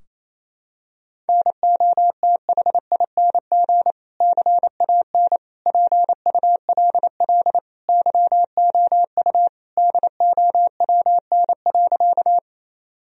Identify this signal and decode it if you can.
Morse code, heard as a keyed tone, about 27 wpm, 700 Hz. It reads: NOTHING CAN PULL YOU DOWN.
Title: Sentences from Top 1000 Words - Review Part 2 - 28wpm